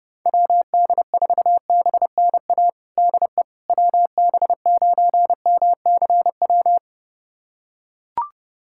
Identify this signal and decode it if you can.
Morse code, heard as a keyed tone, about 30 wpm, 700 Hz. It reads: WD4BNA DE WB9MCW
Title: USA Callsign DE Callsign - 6 Characters - 30wpm